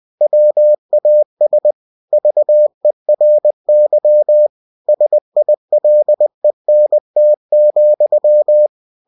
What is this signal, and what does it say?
Morse code, heard as a keyed tone, about 20 wpm, 600 Hz. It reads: WAS VERY SILENT,
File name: cw-books / winnie-the-pooh / winnie-20wpm-page-0107